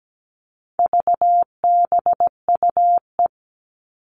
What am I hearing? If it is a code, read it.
Morse code, heard as a keyed tone, about 17 wpm, 700 Hz. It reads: VBUE
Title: All Cut Numbers - 4-Digit 17wpm